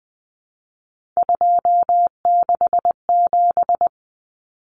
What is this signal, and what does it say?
Morse code, heard as a keyed tone, about 20 wpm, 700 Hz. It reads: 267
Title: Common Cut Numbers - 3-Digit 20wpm